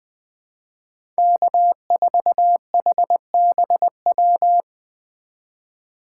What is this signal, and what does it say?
Morse code, heard as a keyed tone, about 20 wpm, 700 Hz. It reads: K4HBW
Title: USA Call Signs - 5 Characters - 2x Character Spacing - 20wpm